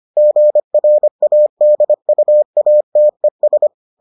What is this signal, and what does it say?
Morse code, heard as a keyed tone, about 25 wpm, 600 Hz. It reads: GRADUATES